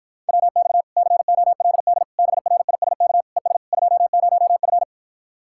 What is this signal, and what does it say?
Morse code, heard as a keyed tone, about 53 wpm, 700 Hz. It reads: WX COLD BRISK ES 20F